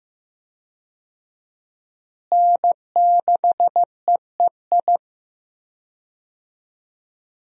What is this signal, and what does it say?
Morse code, heard as a keyed tone, about 15 wpm, 700 Hz. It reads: N6EEI
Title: USA Call Signs - 5 Characters - Repeated 2x - 15wpm